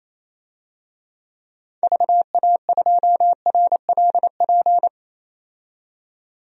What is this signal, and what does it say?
Morse code, heard as a keyed tone, about 28 wpm, 700 Hz. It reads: VA2RLP
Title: DXCC-100 Call Signs - 6 Characters - 28wpm